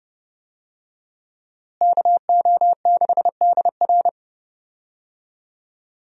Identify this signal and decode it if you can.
Morse code, heard as a keyed tone, about 30 wpm, 700 Hz. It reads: KO6DR
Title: USA Call Signs - 5 Characters - Repeated 2x - 30wpm